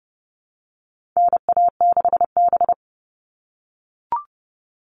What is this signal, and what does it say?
Morse code, heard as a keyed tone, about 30 wpm, 700 Hz. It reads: NA6B
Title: All Cut Numbers - 4-Digit 30wpm